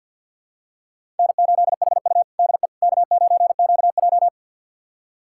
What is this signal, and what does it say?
Morse code, heard as a keyed tone, about 50 wpm, 690 Hz. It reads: N8HU DE K9QJ
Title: USA Callsign DE Callsign - 4 Characters - 50wpm